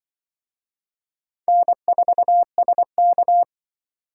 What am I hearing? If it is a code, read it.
Morse code, heard as a keyed tone, about 25 wpm, 700 Hz. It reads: N4SK